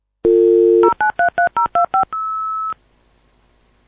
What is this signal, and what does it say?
DTMF, heard as telephone touch tones, tones of about 120 ms. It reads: 0933025